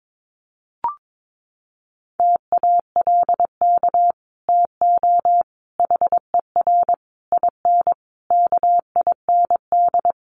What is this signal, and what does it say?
Morse code, heard as a keyed tone, about 22 wpm, 700 Hz. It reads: TALK TO HER IN KIND
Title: Sentences from Top 1500 Words 22wpm